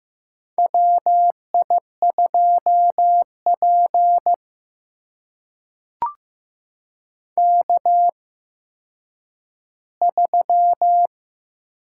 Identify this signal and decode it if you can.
Morse code, heard as a keyed tone, about 15 wpm, spaced as for 9 wpm, 700 Hz. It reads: WI2P K3
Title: USA Call Signs - 4 Characters - 8x Character Spacing - 15wpm